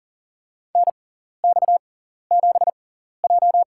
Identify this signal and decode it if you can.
Morse code, heard as a keyed tone, about 40 wpm, spaced as for 14 wpm, 700 Hz. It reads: NX7J
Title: USA Call Signs - 4 Characters - 6x Character Spacing - 40wpm